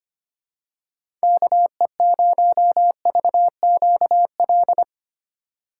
Morse code, heard as a keyed tone, about 25 wpm, 700 Hz. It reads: KE0VQL
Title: USA Call Signs - 6 Characters - 4x Character Spacing - 25wpm